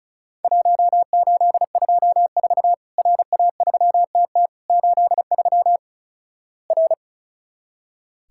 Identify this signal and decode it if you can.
Morse code, heard as a keyed tone, about 34 wpm, 700 Hz. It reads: 1824 RA3TT 83 R
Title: Worked All Europe DX Contest - QTC Exchange - 35wpm